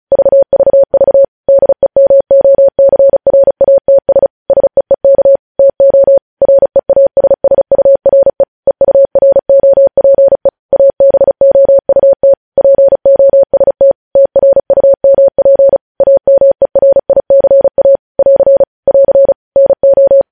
Morse code, heard as a keyed tone, about 36 wpm, 550 Hz. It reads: VVV DEMOCRATS SEEK TO REASSURE EUROPE ABOUT POST TRUMP AMERICA <AR> <AR> NO